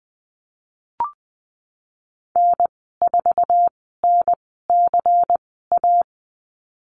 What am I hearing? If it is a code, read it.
Morse code, heard as a keyed tone, about 20 wpm, spaced as for 14 wpm, 700 Hz. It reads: N4NCA
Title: 193 Lesson 31g - Letter v - Call Signs 20wpm